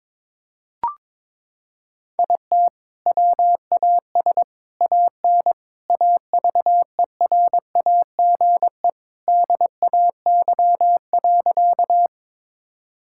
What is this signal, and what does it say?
Morse code, heard as a keyed tone, about 22 wpm, 700 Hz. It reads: IT WAS AN AVERAGE DAY.
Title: Sentences from Top 1000 Words - Review Part 6 - 22wpm